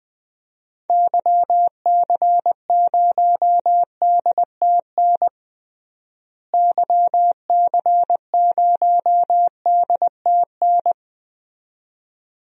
Morse code, heard as a keyed tone, about 20 wpm, 700 Hz. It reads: YC0DTN YC0DTN
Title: DXCC-100 Call Signs - 6 Characters - Repeated 2x - 20wpm